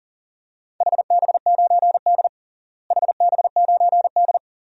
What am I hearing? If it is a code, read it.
Morse code, heard as a keyed tone, about 40 wpm, 700 Hz. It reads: HB9D HB9D
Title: DXCC-100 Call Signs - 4 Characters - Repeated 2x - 40wpm